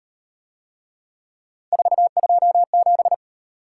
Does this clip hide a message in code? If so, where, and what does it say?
Yes: Morse code from 1.7 to 3.1 s, heard as a keyed tone, about 38 wpm, 700 Hz. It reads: 427